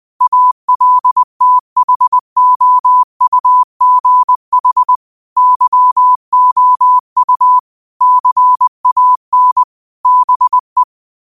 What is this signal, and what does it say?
Morse code, heard as a keyed tone, about 20 wpm, 1000 Hz. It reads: ALTHOUGH YOU CAN BE